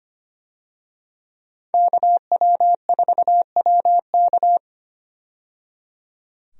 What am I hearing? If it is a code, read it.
Morse code, heard as a keyed tone, about 25 wpm, 700 Hz. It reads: KW4WK
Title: USA Call Signs - 5 Characters - 25wpm